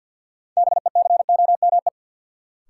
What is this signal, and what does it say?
Morse code, heard as a keyed tone, about 50 wpm, 700 Hz. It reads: BECOME
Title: Sentences from Top 600 Words - New Word Drill - 50wpm